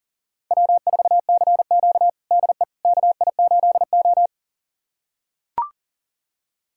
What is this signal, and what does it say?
Morse code, heard as a keyed tone, about 37 wpm, 700 Hz. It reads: W4CQ DE KI8O
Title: USA Callsign DE Callsign - 4 Characters - 40wpm